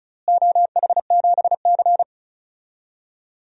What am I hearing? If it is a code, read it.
Morse code, heard as a keyed tone, about 36 wpm, 700 Hz. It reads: OH7C
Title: DXCC-100 Call Signs - 4 Characters - Repeated 2x - 35wpm